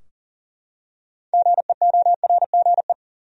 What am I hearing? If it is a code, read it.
Morse code, heard as a keyed tone, about 40 wpm, 700 Hz. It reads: GEORGE
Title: USA Common Names - 40wpm